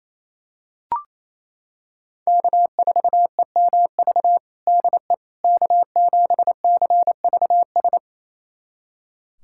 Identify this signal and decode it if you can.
Morse code, heard as a keyed tone, about 27 wpm, 700 Hz. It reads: K4EMV DE K7CVS
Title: USA Callsign DE Callsign - 5 Characters - 28wpm